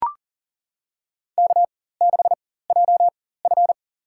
Morse code, heard as a keyed tone, about 37 wpm, spaced as for 18 wpm, 700 Hz. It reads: K6JF
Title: USA Call Signs - 4 Characters - 4x Character Spacing - 40wpm